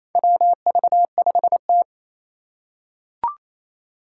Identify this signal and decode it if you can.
Morse code, heard as a keyed tone, about 28 wpm, 700 Hz. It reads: WV5T
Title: Four Challenging Letter-Number Combinations 28wpm